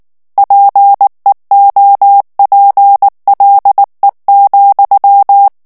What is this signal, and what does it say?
Morse code, heard as a keyed tone, about 19 wpm, 800 Hz. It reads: PEOPLE,
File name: ANGLAIS texte clair 2 vitesse 18 mots minute